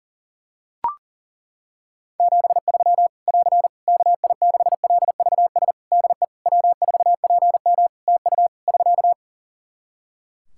Morse code, heard as a keyed tone, about 40 wpm, 700 Hz. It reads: 73 <AR> KI6LVS DE W4PM TU <SK>
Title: Practice QSO - Exchange 4 - Ending - 40wpm